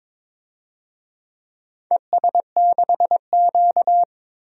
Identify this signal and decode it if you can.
Morse code, heard as a keyed tone, about 22 wpm, 700 Hz. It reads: ES6Q
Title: DXCC-100 Call Signs - 4 Characters - 22wpm